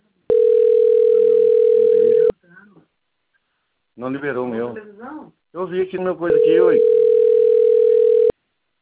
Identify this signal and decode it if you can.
Ringback tone (North America), 440 + 480 Hz, with speech.
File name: rg-1600-unknown- - - .51070